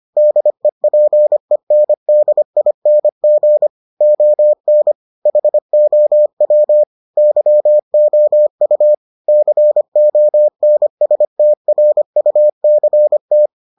Morse code, heard as a keyed tone, about 25 wpm, 600 Hz. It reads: DEPENDING ON HOW YOU CONSTRUCT